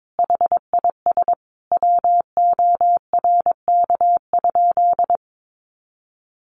Morse code, heard as a keyed tone, about 22 wpm, 700 Hz. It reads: HIS WORK?
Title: Sentences from Top 1300 Words 22wpm